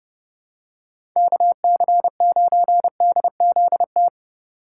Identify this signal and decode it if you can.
Morse code, heard as a keyed tone, about 28 wpm, 700 Hz. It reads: KC9DZT